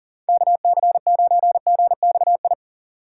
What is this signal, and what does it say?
Morse code, heard as a keyed tone, about 52 wpm, 700 Hz. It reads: KC9GXI